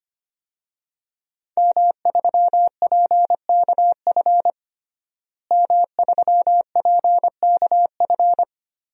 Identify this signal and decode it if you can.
Morse code, heard as a keyed tone, about 25 wpm, 700 Hz. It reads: M3PKF M3PKF
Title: DXCC-100 Call Signs - 5 Characters - Repeated 3x - 25wpm